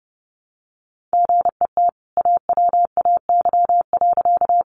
Morse code, heard as a keyed tone, about 30 wpm, 700 Hz. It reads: GET AWAY.
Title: Sentences from Top 600 Words 30wpm